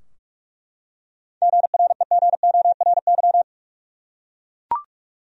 Morse code, heard as a keyed tone, about 42 wpm, 700 Hz. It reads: GREGORY
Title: USA Common Names - 45wpm